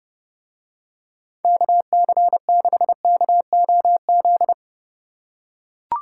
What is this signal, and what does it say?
Morse code, heard as a keyed tone, about 29 wpm, 700 Hz. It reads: KC6KOZ